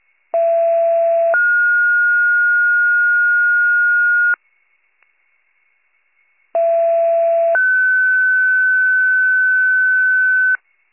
Two-tone page, heard as a one-second tone, then a three-second tone, about 670 then 1430 Hz, about 670 then 1500 Hz. It is heard from 0.3 to 10.6 s.